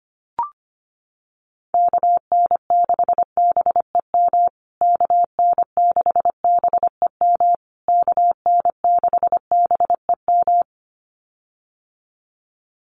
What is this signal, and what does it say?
Morse code, heard as a keyed tone, about 25 wpm, 700 Hz. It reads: KN6BEM KN6BEM KN6BEM
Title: USA Call Signs Repeated 3x 25wpm